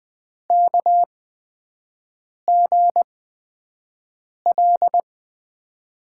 Morse code, heard as a keyed tone, about 20 wpm, spaced as for 5 wpm, 700 Hz. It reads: KGL
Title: USA Call Signs - 6 Characters - 8x Character Spacing - 20wpm